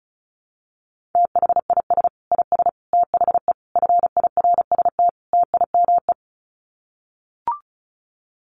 Morse code, heard as a keyed tone, about 34 wpm, 700 Hz. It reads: THIS IS THE FIRST TIME
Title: Sets of 5 Words 35wpm